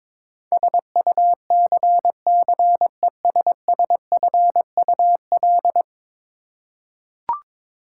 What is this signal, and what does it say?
Morse code, heard as a keyed tone, about 22 wpm, 700 Hz. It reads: SUCCESSFUL
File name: practice / 22wpm / 10 Letter Words - First 3 Letters Given - 22wpm